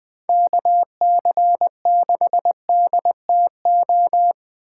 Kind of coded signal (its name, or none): Morse code